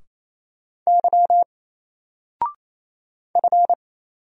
Morse code, heard as a keyed tone, about 29 wpm, 700 Hz. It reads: Y F
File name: Single-Letter-Number-28wpm